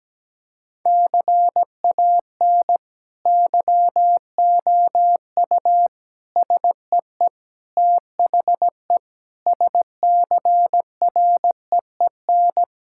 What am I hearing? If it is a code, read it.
Morse code, heard as a keyed tone, about 17 wpm, 700 Hz. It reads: CAN YOU SEE THE SCREEN
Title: Sentences from Top 1000 Words - Review Part 6 - 17wpm